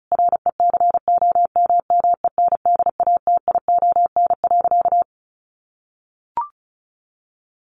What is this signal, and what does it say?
Morse code, heard as a keyed tone, about 34 wpm, 700 Hz. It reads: RECOMMENDATION.
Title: Sentences from Top 1400 Words 35wpm